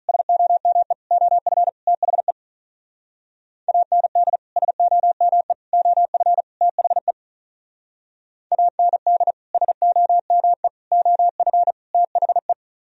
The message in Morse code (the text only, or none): SOME OF THE AND SOME OF THE AND SOME OF THE